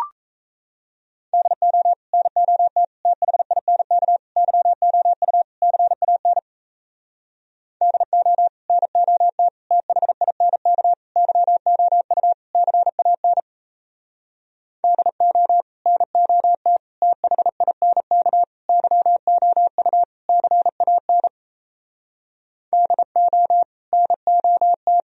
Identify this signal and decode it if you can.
Morse code, heard as a keyed tone, about 37 wpm, 700 Hz. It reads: DO NOT THINK YOU CAN DO NOT THINK YOU CAN DO NOT THINK YOU CAN DO NOT